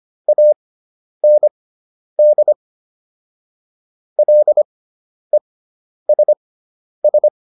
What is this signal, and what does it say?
Morse code, heard as a keyed tone, about 25 wpm, spaced as for 10 wpm, 600 Hz. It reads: AND LESS